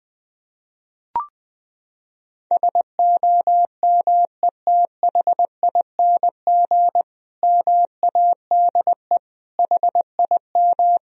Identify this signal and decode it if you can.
Morse code, heard as a keyed tone, about 20 wpm, 700 Hz. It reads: SOMETHING MADE HIM
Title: Sentences from Top 1000 Words - Review Part 1 - 20wpm